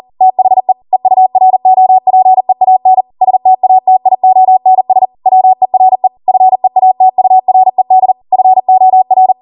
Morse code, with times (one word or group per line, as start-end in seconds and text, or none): 0.2-0.7 THE
0.9-3.0 EUROPEAN
3.2-5.0 STATIONS
5.3-6.1 WERE
6.3-8.1 FEATURED
8.3-9.3 FOR